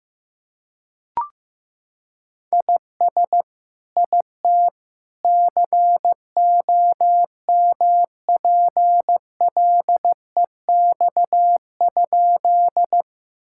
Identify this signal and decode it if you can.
Morse code, heard as a keyed tone, about 15 wpm, 700 Hz. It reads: IS IT COMPLEX?
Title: Sentences from Top 1000 Words - Review Part 6 - 15wpm